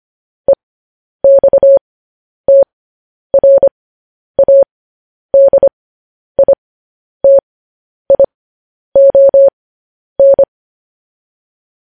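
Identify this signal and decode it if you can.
Morse code, heard as a keyed tone, about 25 wpm, spaced as for 10 wpm, 550 Hz. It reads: EXTRADITION
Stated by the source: Morse code news headlines sent at 10 wpm for practice.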